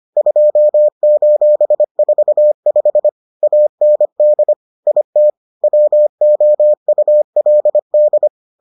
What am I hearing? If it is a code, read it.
Morse code, heard as a keyed tone, about 25 wpm, 600 Hz. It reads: 2:45 AND IT WOULD